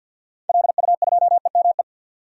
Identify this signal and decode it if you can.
Morse code, heard as a keyed tone, about 49 wpm, 700 Hz. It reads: LU2EME
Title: DXCC-100 Call Signs - 6 Characters - 50wpm